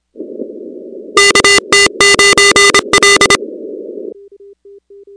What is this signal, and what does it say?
Morse code, heard as a keyed tone, about 26 wpm, 2000 Hz, with noise. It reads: KT9L
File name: contests / 2019-cw-wpx-kt5j-k5pi- -2330 0-13